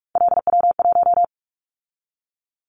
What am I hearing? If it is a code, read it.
Morse code, heard as a keyed tone, about 44 wpm, 700 Hz. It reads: LW1
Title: Callsign Prefixes for Top 100 DXCC - 45wpm